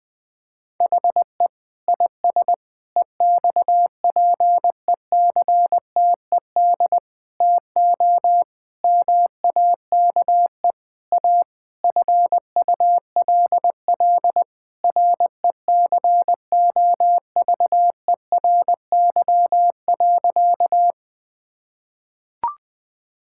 Morse code, heard as a keyed tone, about 20 wpm, 700 Hz. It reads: HE IS EXPECTED TO MAKE A FULL RECOVERY.
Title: Sentences with 7-Word Phrases - Spoken Beginning - 20wpm